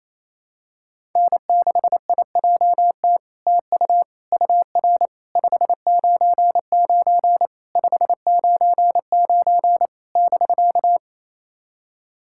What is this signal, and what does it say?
Morse code, heard as a keyed tone, about 28 wpm, 700 Hz. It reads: N6IJT TU UR 599 599 <BK>
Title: SOTA QSO Practice 28wpm